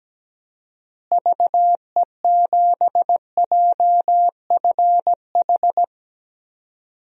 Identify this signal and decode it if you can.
Morse code, heard as a keyed tone, about 17 wpm, 700 Hz. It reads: VE7JFH
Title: DXCC-100 Call Signs - 6 Characters - Repeated 2x - 17wpm